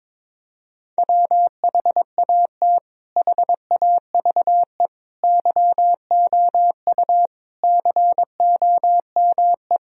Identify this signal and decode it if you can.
Morse code, heard as a keyed tone, about 22 wpm, 700 Hz. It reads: WHAT HAVE YOU COME
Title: Sentences from Top 1000 Words - Review Part 2 - 22wpm